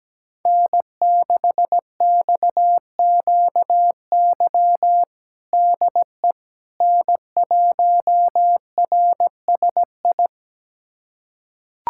Morse code, heard as a keyed tone, about 17 wpm, 700 Hz. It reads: N6XQY DE N1RSI
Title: USA Callsign DE Callsign - 5 Characters - 17wpm